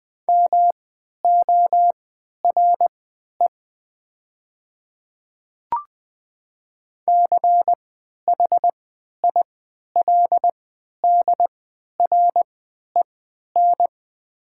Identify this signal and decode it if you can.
Morse code, heard as a keyed tone, about 20 wpm, spaced as for 11 wpm, 700 Hz. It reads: MORE CHILDREN
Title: 3x Character Spacing - Top 300 Words - 20wpm